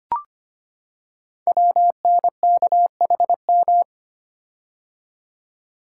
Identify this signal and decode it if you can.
Morse code, heard as a keyed tone, about 25 wpm, 700 Hz. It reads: WNKHM
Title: Code Groups - 5 Letters - 25wpm